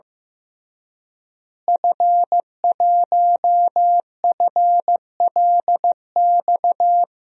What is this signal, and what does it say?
Morse code, heard as a keyed tone, about 15 wpm, 700 Hz. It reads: F1FLX
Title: DXCC-100 Call Signs - 5 Characters - Repeated 3x - 15wpm